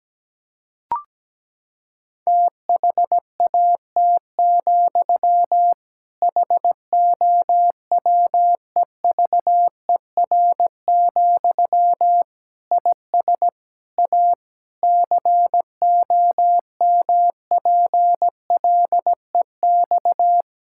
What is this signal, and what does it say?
Morse code, heard as a keyed tone, about 17 wpm, 700 Hz. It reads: THAT, HOWEVER, IS A COMPLEX
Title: Sentences from Top 1000 Words - Review Part 6 - 17wpm